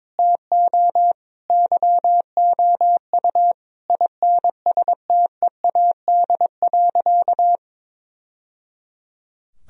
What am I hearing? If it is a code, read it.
Morse code, heard as a keyed tone, about 22 wpm, 700 Hz. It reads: TO YOU INSTEAD.
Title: Sentences from Top 1000 Words - Review Part 4 - 22wpm